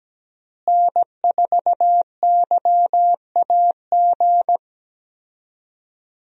Morse code, heard as a keyed tone, about 17 wpm, 700 Hz. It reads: N4YAG
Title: USA Call Signs - 5 Characters - Repeated 3x - 17wpm